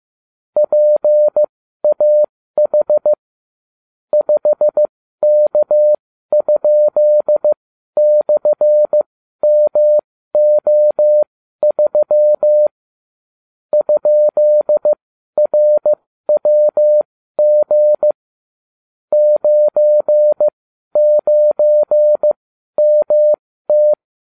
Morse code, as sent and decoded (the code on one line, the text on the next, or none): .--. .- .... / ..... -.- ..--.. -..-. -- --- ...-- / ..--.. .-. .-- --. / ----. ----. -- -
PAH 5K?/MO3 ?RWG 99MT